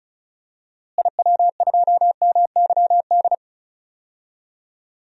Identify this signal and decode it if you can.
Morse code, heard as a keyed tone, about 36 wpm, 700 Hz. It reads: IW2MYD
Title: DXCC-100 Call Signs - 6 Characters - Repeated 2x - 35wpm